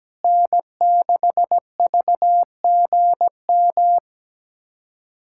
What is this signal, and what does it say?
Morse code, heard as a keyed tone, about 17 wpm, 700 Hz. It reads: N6VGM
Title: USA Callsign DE Callsign - 5 Characters - 17wpm